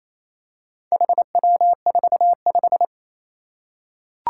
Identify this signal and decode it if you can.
Morse code, heard as a keyed tone, about 27 wpm, 700 Hz. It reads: HW45
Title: Four Challenging Letter-Number Combinations 28wpm